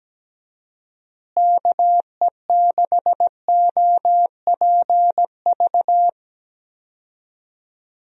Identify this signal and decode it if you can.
Morse code, heard as a keyed tone, about 17 wpm, 700 Hz. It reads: KE6OPV
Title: USA Call Signs - 6 Characters - Repeated 2x - 17wpm